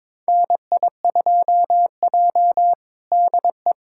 Morse code, heard as a keyed tone, about 22 wpm, 700 Hz. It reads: NI2J DE
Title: USA Callsign DE Callsign - 4 Characters - 22wpm